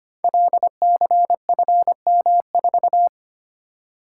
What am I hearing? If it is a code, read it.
Morse code, heard as a keyed tone, about 25 wpm, 700 Hz. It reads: LCFM4
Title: Code Groups - 5 Letter:Numbers - 25wpm